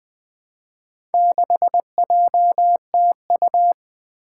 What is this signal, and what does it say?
Morse code, heard as a keyed tone, about 20 wpm, 700 Hz. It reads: 6JTU
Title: Four Challenging Letter-Number Combinations 20wpm